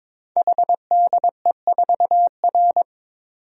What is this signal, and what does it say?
Morse code, heard as a keyed tone, about 22 wpm, 700 Hz. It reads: HDE4R
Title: Code Groups - 5 Letter:Numbers - 22wpm